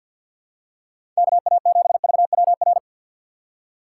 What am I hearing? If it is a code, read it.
Morse code, heard as a keyed tone, about 52 wpm, 700 Hz. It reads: KA7VWR
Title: USA Call Signs - 6 Characters - 3x Character Spacing - 50wpm